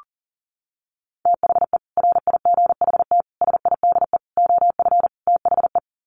Morse code, heard as a keyed tone, about 40 wpm, 700 Hz. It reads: THE RIGHT SIDE OF THE
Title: Sets of 5 Words Encore 40wpm